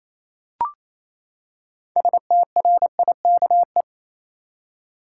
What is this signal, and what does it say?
Morse code, heard as a keyed tone, about 28 wpm, 700 Hz. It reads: STRIKE
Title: Six Letter Words - 28wpm